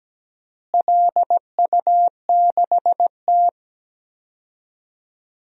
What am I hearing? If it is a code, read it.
Morse code, heard as a keyed tone, about 17 wpm, 700 Hz. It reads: LU6T
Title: DXCC-100 Call Signs - 4 Characters - Repeated 2x - 17wpm